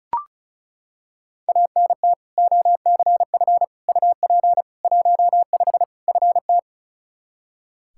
Morse code, heard as a keyed tone, about 34 wpm, 700 Hz. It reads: ANT OCF UP 15 FT
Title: Practice QSO - Exchange 2 - Info - 35wpm